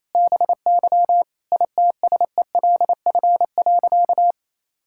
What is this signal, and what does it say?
Morse code, heard as a keyed tone, about 28 wpm, 700 Hz. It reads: BY ITSELF.
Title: Sentences from Top 1000 Words - Review Part 5 - 28wpm